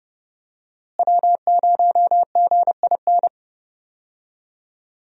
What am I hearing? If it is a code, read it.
Morse code, heard as a keyed tone, about 30 wpm, 700 Hz. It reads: W0GIN
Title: USA Call Signs - 5 Characters - Repeated 2x - 30wpm